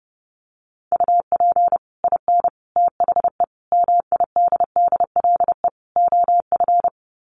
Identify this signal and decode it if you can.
Morse code, heard as a keyed tone, about 30 wpm, 700 Hz. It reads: UP IN THE MIDDLE OF
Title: Sets of 5 Words Encore 30wpm